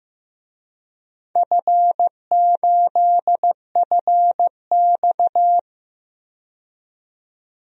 Morse code, heard as a keyed tone, about 15 wpm, 700 Hz. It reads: F8FX
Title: DXCC-100 Call Signs - 4 Characters - 15wpm